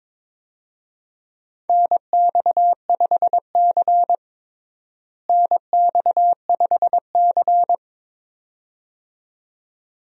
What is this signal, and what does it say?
Morse code, heard as a keyed tone, about 22 wpm, 700 Hz. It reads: NX5C NX5C
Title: USA Call Signs - 4 Characters - Repeated 2x - 22wpm